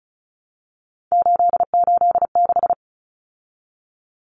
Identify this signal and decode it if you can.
Morse code, heard as a keyed tone, about 34 wpm, 700 Hz. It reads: 886
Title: Numbers - 3-Digit 35wpm